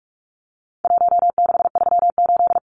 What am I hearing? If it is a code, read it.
Morse code, heard as a keyed tone, about 44 wpm, 700 Hz. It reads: 1638
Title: Numbers - Contest Style - 4-Digit 45wpm